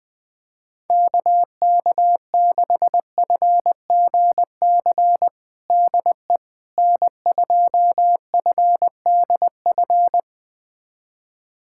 Morse code, heard as a keyed tone, about 20 wpm, 700 Hz. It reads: KK6FGC DE N2FDF
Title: USA Callsign DE Callsign 20wpm